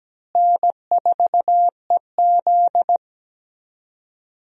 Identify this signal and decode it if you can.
Morse code, heard as a keyed tone, about 17 wpm, 700 Hz. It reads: N4EZ
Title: USA Call Signs - 4 Characters - Repeated 3x - 17wpm